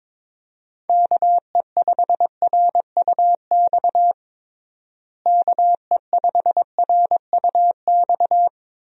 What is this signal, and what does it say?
Morse code, heard as a keyed tone, about 22 wpm, 700 Hz. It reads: KE5RUX KE5RUX
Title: USA Call Signs - 6 Characters - Repeated 2x - 22wpm